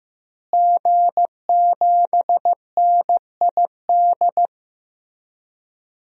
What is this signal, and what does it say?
Morse code, heard as a keyed tone, about 15 wpm, 700 Hz. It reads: G7NID
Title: DXCC-100 Call Signs - 5 Characters - Repeated 3x - 15wpm